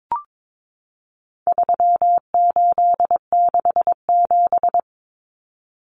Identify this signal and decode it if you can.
Morse code, heard as a keyed tone, about 22 wpm, 700 Hz. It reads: 3867
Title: Numbers - 4-Digit 22wpm